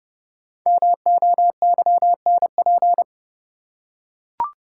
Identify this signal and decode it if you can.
Morse code, heard as a keyed tone, about 32 wpm, 700 Hz. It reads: MOYNP
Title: Code Groups - 5 Letter:Numbers - 30wpm